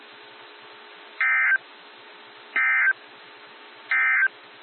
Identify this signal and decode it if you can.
Emergency Alert System (SAME) tones, heard as harsh buzzing bursts. It is heard from 1.1 to 4.4 s.
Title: NOAA Weather Radio - WXJ73